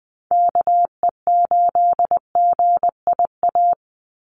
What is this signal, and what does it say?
Morse code, heard as a keyed tone, about 19 wpm, 700 Hz. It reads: KE8GIA
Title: USA Call Signs 20wpm